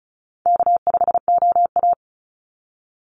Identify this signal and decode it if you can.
Morse code, heard as a keyed tone, about 34 wpm, 700 Hz. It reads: K5OA
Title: Easy Call Signs Repeated 2x - 35wpm